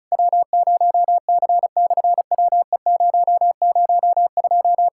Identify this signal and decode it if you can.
Morse code, heard as a keyed tone, about 36 wpm, 700 Hz. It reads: W0C/WE002
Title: SOTA QSO Practice 35wpm